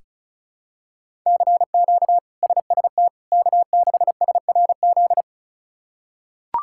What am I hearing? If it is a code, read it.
Morse code, heard as a keyed tone, about 34 wpm, 700 Hz. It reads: CQ SST K6SRZ
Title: K1USN SST QSO Practice - 35wpm